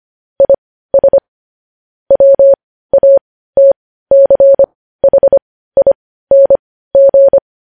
Morse code, heard as a keyed tone, about 25 wpm, spaced as for 15 wpm, 550 Hz. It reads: IS WATCHING
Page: news headlines in morse code at 15 wpm